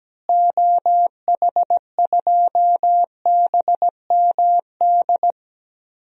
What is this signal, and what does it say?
Morse code, heard as a keyed tone, about 17 wpm, 700 Hz. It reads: OH2BMD
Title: DXCC-100 Call Signs - 6 Characters - Repeated 3x - 17wpm